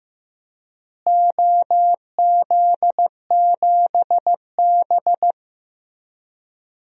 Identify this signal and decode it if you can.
Morse code, heard as a keyed tone, about 15 wpm, 700 Hz. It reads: OZ7B
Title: DXCC-100 Call Signs - 4 Characters - Repeated 3x - 15wpm